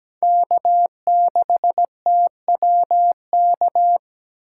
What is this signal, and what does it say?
Morse code, heard as a keyed tone, about 20 wpm, 700 Hz. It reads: K6TWK
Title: Speed Racing - USA 1x3 Call Signs 17wpm